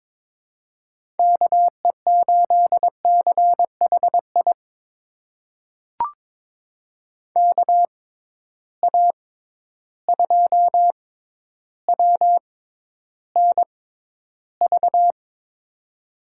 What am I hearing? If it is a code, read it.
Morse code, heard as a keyed tone, about 22 wpm, spaced as for 11 wpm, 700 Hz. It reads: KE8CHI KA2WNV
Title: USA Call Signs - 6 Characters - 6x Character Spacing - 22wpm